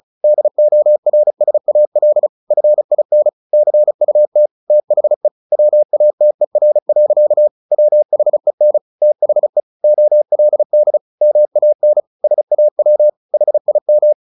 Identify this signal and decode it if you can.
Morse code, heard as a keyed tone, about 34 wpm, 600 Hz. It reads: DORSAL FIN CUT THE WATER. WHEN THE OLD MAN SAW HIM
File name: the old man and the sea 2 35wpm